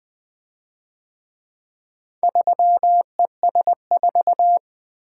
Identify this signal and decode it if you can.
Morse code, heard as a keyed tone, about 24 wpm, 700 Hz. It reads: 3ES4
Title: Speed-Racing - Four Letter-Number Combinations - 20wpm